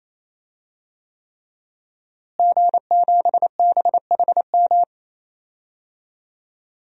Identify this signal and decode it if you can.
Morse code, heard as a keyed tone, about 28 wpm, 700 Hz. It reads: G7BHM
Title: DXCC-100 Call Signs - 5 Characters - Repeated 3x - 28wpm